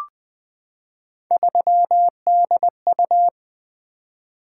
Morse code, heard as a keyed tone, about 20 wpm, 700 Hz. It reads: 3DU
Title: Three Letter-Number Combinations - 20wpm